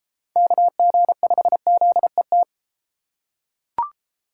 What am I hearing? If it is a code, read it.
Morse code, heard as a keyed tone, about 31 wpm, 700 Hz. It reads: KG5ZET